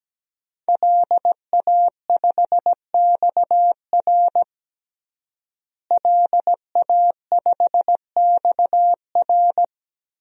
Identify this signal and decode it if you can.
Morse code, heard as a keyed tone, about 17 wpm, 700 Hz. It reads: LA5XR LA5XR
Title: DXCC-100 Call Signs - 5 Characters - Repeated 3x - 17wpm